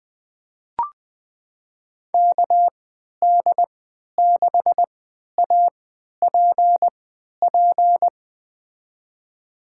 Morse code, heard as a keyed tone, about 20 wpm, spaced as for 11 wpm, 700 Hz. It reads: KD6APP
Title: USA Call Signs - 6 Characters - 3x Character Spacing - 20wpm